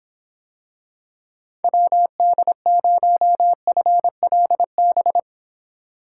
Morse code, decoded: WD0FLB

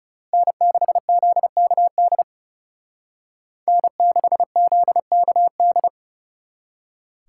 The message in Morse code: N6ZKD N6ZKD

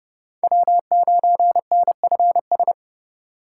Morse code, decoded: W9NFS

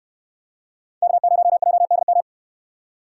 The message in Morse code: K0JDM

